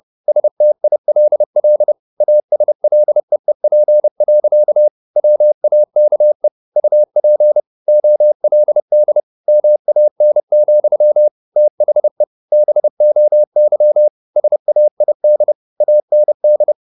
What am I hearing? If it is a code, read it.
Morse code, heard as a keyed tone, about 30 wpm, 600 Hz. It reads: STILL ASLEEP. WAKE UP OLD MAN, THE BOY SAID AND
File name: the old man and the sea / the old man and the sea 1 30wpm